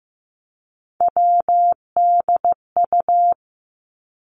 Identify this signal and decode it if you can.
Morse code, heard as a keyed tone, about 16 wpm, 700 Hz. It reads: WDU
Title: Three Challenging Letter-Number Combinations 15wpm